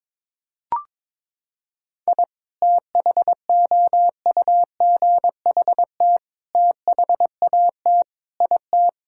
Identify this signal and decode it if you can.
Morse code, heard as a keyed tone, about 22 wpm, 700 Hz. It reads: I THOUGHT THAT IT
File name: practice / 22wpm / Sentences with 4-Word Phrases - 22wpm